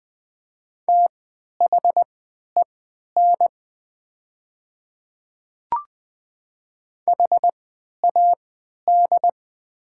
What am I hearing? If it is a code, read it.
Morse code, heard as a keyed tone, about 20 wpm, spaced as for 11 wpm, 700 Hz. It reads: THEN HAD
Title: 3x Character Spacing - Top 100 Words - 20wpm